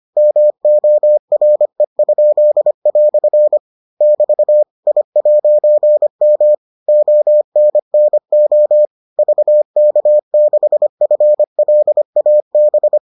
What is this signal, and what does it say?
Morse code, heard as a keyed tone, about 25 wpm, 600 Hz. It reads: MORE?" <BT> I'M ONNO VK6FLAB